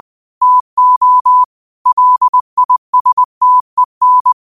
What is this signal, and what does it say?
Morse code, heard as a keyed tone, about 20 wpm, 1000 Hz. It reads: TO LISTEN
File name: qotdfiles / 20wpm